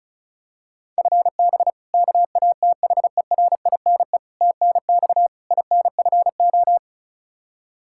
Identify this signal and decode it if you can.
Morse code, heard as a keyed tone, about 36 wpm, 700 Hz. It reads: FB KATHERINE TNX INFO
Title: Practice QSO - Exchange 3 - More Info - 35wpm